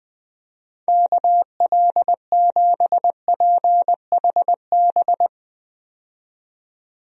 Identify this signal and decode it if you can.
Morse code, heard as a keyed tone, about 20 wpm, 700 Hz. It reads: KL7PHB
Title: DXCC-100 Call Signs - 6 Characters - 20wpm